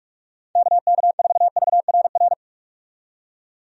Morse code, heard as a keyed tone, about 46 wpm, 700 Hz. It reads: KK4VRR
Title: USA Call Signs - 6 Characters - 45wpm